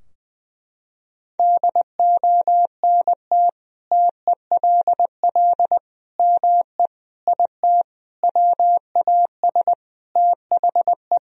Morse code, heard as a keyed tone, about 20 wpm, 700 Hz. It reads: DONT TELL ME IT WAS THE
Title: Sentences from Top 1000 Words - Review Part 3 - 20wpm